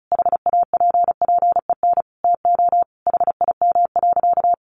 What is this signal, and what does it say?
Morse code, heard as a keyed tone, about 34 wpm, 700 Hz. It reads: HAPPEN TO HIM.
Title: Sentences from Top 400 Words 35wpm